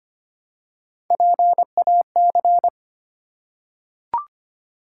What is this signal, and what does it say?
Morse code, heard as a keyed tone, about 25 wpm, 700 Hz. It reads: PAC
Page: morse-code-ninja-practice-25wpm